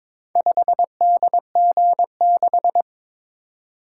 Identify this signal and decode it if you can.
Morse code, heard as a keyed tone, about 22 wpm, 700 Hz. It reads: 5DG6
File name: practice / 22wpm / Four Letter-Number Combinations - 22wpm